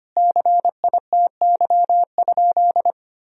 Morse code, heard as a keyed tone, about 25 wpm, 700 Hz. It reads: CITY?
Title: Sentences from Top 1000 Words - Review Part 1 - 25wpm